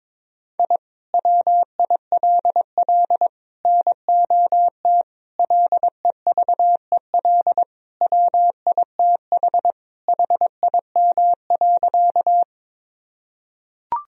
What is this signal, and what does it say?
Morse code, heard as a keyed tone, about 22 wpm, 700 Hz. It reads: I WILL NOT LEVEL WITH HIM.
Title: Sentences from Top 1000 Words - Review Part 4 - 22wpm